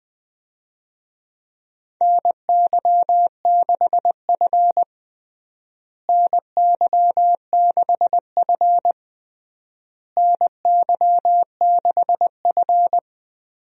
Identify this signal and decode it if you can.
Morse code, heard as a keyed tone, about 20 wpm, 700 Hz. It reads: NY6F NY6F NY6F
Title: USA Call Signs - 4 Characters - Repeated 3x - 20wpm